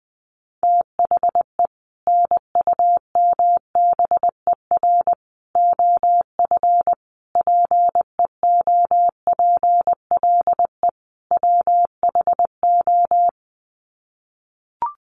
Morse code, heard as a keyed tone, about 20 wpm, 700 Hz. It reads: THE NUMBER OF PEOPLE WHO
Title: Sets of 5 Words Encore 20wpm